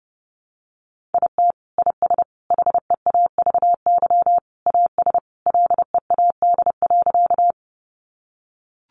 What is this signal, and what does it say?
Morse code, heard as a keyed tone, about 30 wpm, 700 Hz. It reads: IT IS HEAVY AS LEAD.
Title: Sentences from Top 1100 Words 30wpm